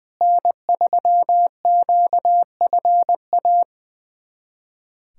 Morse code, heard as a keyed tone, about 21 wpm, 700 Hz. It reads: N3QFA
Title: USA Callsign DE Callsign - 5 Characters - 20wpm